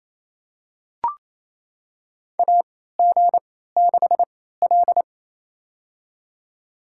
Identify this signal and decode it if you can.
Morse code, heard as a keyed tone, about 28 wpm, spaced as for 16 wpm, 700 Hz. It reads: AG6L
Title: USA Call Signs - 4 Characters - 3x Character Spacing - 28wpm